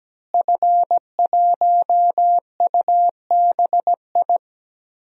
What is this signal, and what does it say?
Morse code, heard as a keyed tone, about 17 wpm, 700 Hz. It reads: F1UBI
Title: DXCC-100 Call Signs - 5 Characters - Repeated 3x - 17wpm